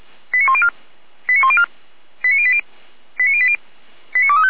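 Five-tone selective call (ZVEI), heard as quick runs of five tones, tones of about 70 ms. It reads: 89104 89104 89080 89080 89235
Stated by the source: Five-tone ZVEI selective call received on an SDR.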